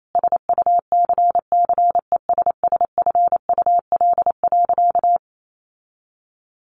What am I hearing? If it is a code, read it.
Morse code, heard as a keyed tone, about 28 wpm, 700 Hz. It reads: SUCCESSFUL.
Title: Sentences from Top 1100 Words 28wpm